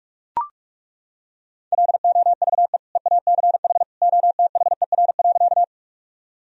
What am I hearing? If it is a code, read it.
Morse code, heard as a keyed tone, about 41 wpm, 700 Hz. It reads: LOVE EACH OTHER.
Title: Sentences from Top 1000 Words - Review Part 1 - 45wpm